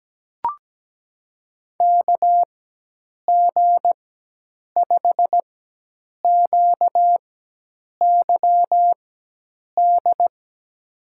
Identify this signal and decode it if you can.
Morse code, heard as a keyed tone, about 17 wpm, spaced as for 8 wpm, 700 Hz. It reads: KG5QYD
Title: USA Call Signs - 6 Characters - 4x Character Spacing - 17wpm